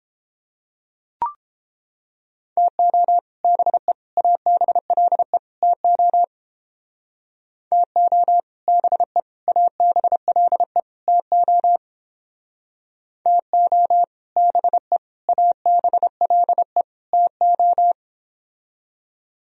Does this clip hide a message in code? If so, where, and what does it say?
Yes: Morse code from 2.6 to 17.9 s, heard as a keyed tone, about 29 wpm, 700 Hz. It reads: TO BE ABLE TO TO BE ABLE TO TO BE ABLE TO